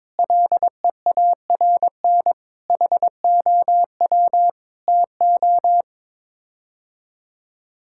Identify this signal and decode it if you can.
Morse code, heard as a keyed tone, about 22 wpm, 700 Hz. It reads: LEARN HOW TO
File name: practice / 22wpm / Sets of 7 Words - 22wpm